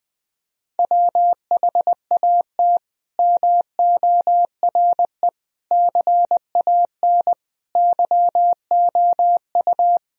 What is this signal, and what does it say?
Morse code, heard as a keyed tone, about 20 wpm, 700 Hz. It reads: WHAT MORE CAN YOU
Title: Sentences from Top 1000 Words - Review Part 4 - 20wpm